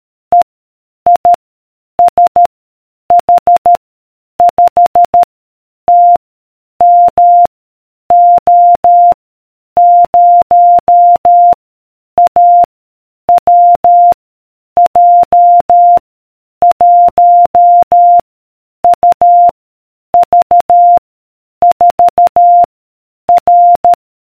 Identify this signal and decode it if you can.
Morse code, heard as a keyed tone, about 13 wpm, 700 Hz. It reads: E I S H 5 T M O 0 A W J 1 U V 4 R